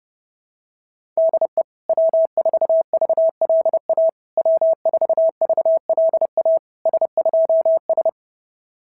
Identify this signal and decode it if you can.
Morse code, heard as a keyed tone, about 30 wpm, 650 Hz. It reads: DE W4VLA W4VLA S2S
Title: SOTA QSO Practice 30wpm